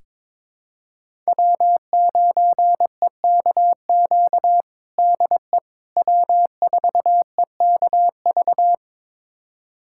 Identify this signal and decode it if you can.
Morse code, heard as a keyed tone, about 22 wpm, 700 Hz. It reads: W9EKQ DE W4EKV
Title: USA Callsign DE Callsign - 5 Characters - 22wpm